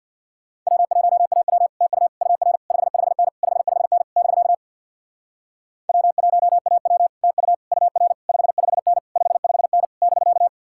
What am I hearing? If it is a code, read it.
Morse code, heard as a keyed tone, about 63 wpm, 700 Hz. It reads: W1AW TU UR 55N 55N <BK> W1AW TU UR 55N 55N <BK>